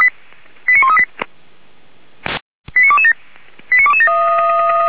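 Five-tone selective call (ZVEI), heard as quick runs of five tones, tones of about 70 ms. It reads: 89138 89227 89227
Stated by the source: ZVEI five-tone call received by SDR, from audio.f85360.feuerwehr-orlamuende.org (Feuerwehr Orlamünde fire brigade server).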